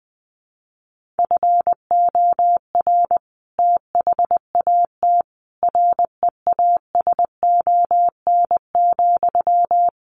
Morse code, heard as a keyed tone, about 20 wpm, 700 Hz. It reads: FOR THAT REASON,
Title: Sentences from Top 1500 Words 20wpm